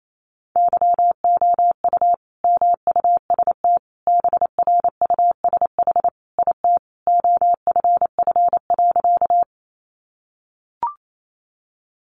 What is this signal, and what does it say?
Morse code, heard as a keyed tone, about 28 wpm, 700 Hz. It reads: YOU MUST BRUSH IT OFF.
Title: Sentences from Top 1100 Words 28wpm